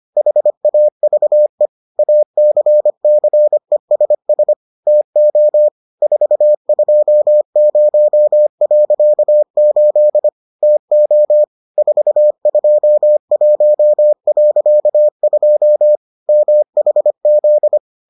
Morse code, heard as a keyed tone, about 25 wpm, 600 Hz. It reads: HAVE ACCESS TO 420.8 TO 421.2 MHZ